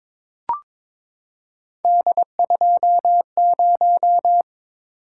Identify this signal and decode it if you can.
Morse code, heard as a keyed tone, about 22 wpm, 700 Hz. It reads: D20
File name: practice / 22wpm / Three Letter-Number Combinations - 22wpm